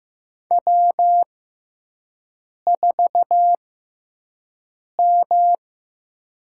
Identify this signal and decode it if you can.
Morse code, heard as a keyed tone, about 15 wpm, spaced as for 5 wpm, 700 Hz. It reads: W4M
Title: USA Call Signs - 4 Characters - 6x Character Spacing - 15wpm